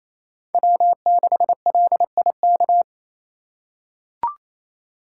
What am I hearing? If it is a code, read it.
Morse code, heard as a keyed tone, about 27 wpm, 700 Hz. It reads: W6LIK
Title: USA Call Signs - 4x Character Spacing - 28wpm